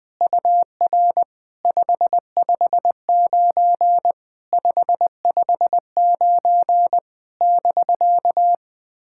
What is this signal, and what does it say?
Morse code, heard as a keyed tone, about 21 wpm, 700 Hz. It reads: UR 559 559 <BK>
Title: POTA QSO Practice - 20wpm